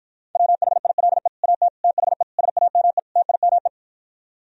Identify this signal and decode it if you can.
Morse code, heard as a keyed tone, about 52 wpm, 700 Hz. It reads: WHILE AT THE SAME TIME